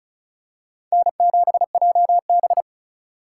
Morse code, heard as a keyed tone, about 34 wpm, 690 Hz. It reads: N7JB
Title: USA Call Signs - 4x Character Spacing - 35wpm